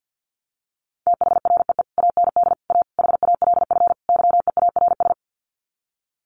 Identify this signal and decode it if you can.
Morse code, heard as a keyed tone, about 48 wpm, 700 Hz. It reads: THREE AND A HALF YEARS
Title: Sets of 5 Words Encore2 50wpm